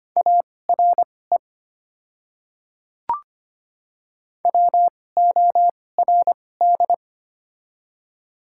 Morse code, heard as a keyed tone, about 25 wpm, spaced as for 18 wpm, 700 Hz. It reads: ARE WORD
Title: 2x Character Spacing - Top 100 Words - 25wpm